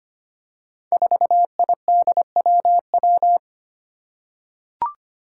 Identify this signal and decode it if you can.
Morse code, heard as a keyed tone, about 25 wpm, 700 Hz. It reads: 4IDWW
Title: Five Challenging Letter-Number Combinations 25wpm